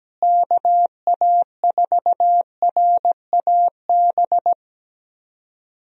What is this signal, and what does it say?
Morse code, heard as a keyed tone, about 17 wpm, 700 Hz. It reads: KA4RAB
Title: USA Call Signs - 6 Characters - Repeated 2x - 17wpm